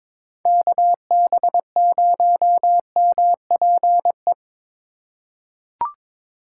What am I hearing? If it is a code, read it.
Morse code, heard as a keyed tone, about 22 wpm, 700 Hz. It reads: KB0MPE
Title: USA Call Signs - 6 Characters - 3x Character Spacing - 22wpm